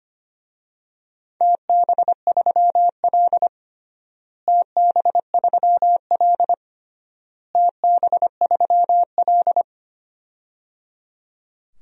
Morse code, heard as a keyed tone, about 25 wpm, 700 Hz. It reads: TB3L TB3L TB3L
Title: DXCC-100 Call Signs - 4 Characters - Repeated 3x - 25wpm